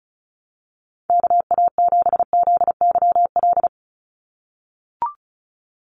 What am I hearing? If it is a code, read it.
Morse code, heard as a keyed tone, about 34 wpm, 700 Hz. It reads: KA7ZYL
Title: USA Call Signs 35wpm